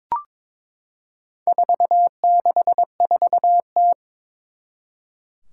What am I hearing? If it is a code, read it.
Morse code, heard as a keyed tone, about 22 wpm, 700 Hz. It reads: 464T
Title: Four Challenging Letter-Number Combinations 22wpm